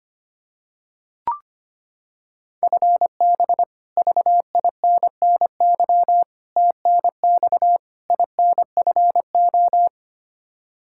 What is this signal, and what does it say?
Morse code, heard as a keyed tone, about 25 wpm, 700 Hz. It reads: FB VINNY TNX INFO
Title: Practice QSO - Exchange 3 - More Info - 25wpm